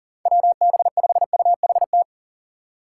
Morse code, heard as a keyed tone, about 40 wpm, 700 Hz. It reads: WB5UHT
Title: USA Call Signs - 6 Characters - 2x Character Spacing - 40wpm